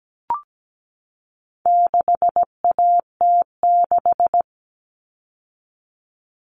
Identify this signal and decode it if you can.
Morse code, heard as a keyed tone, about 17 wpm, 700 Hz. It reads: 6AT6
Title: All Cut Numbers - 4-Digit 17wpm